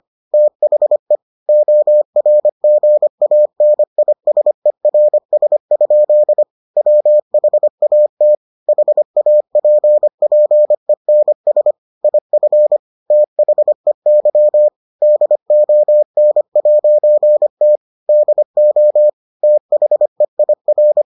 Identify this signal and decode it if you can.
Morse code, heard as a keyed tone, about 25 wpm, 600 Hz. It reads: THE ORGANISERS? WHAT HAPPENS IF THEY DON'T DO THEIR